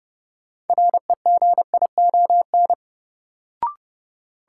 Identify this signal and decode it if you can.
Morse code, heard as a keyed tone, about 30 wpm, 700 Hz. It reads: REGION
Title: Sentences from Top 1500 Words - New Word Drill - 30wpm